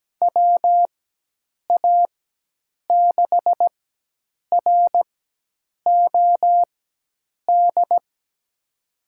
Morse code, heard as a keyed tone, about 17 wpm, spaced as for 8 wpm, 700 Hz. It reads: WA6ROD
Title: USA Call Signs - 6 Characters - 4x Character Spacing - 17wpm